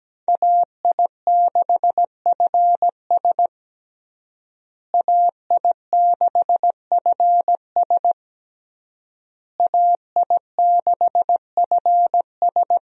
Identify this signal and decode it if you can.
Morse code, heard as a keyed tone, about 17 wpm, 700 Hz. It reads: AI6FS AI6FS AI6FS
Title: USA Call Signs - 5 Characters - Repeated 3x - 17wpm